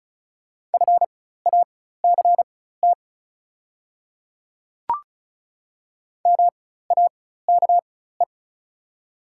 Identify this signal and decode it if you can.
Morse code, heard as a keyed tone, about 36 wpm, spaced as for 16 wpm, 690 Hz. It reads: FACT MAKE
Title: 4x Character Spacing - Top 500 Words - 35wpm